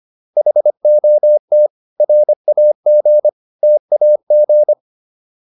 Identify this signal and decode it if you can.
Morse code, heard as a keyed tone, about 25 wpm, 600 Hz. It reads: HOT RAG TAG